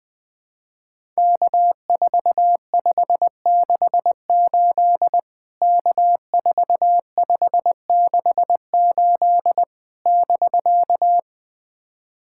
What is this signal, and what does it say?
Morse code, heard as a keyed tone, about 20 wpm, 700 Hz. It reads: K4568 K4568 <BK>
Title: POTA QSO Practice - 20wpm